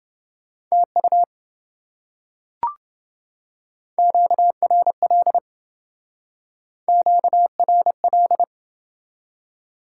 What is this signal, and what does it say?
Morse code, heard as a keyed tone, about 28 wpm, 700 Hz. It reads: TU QRL QRL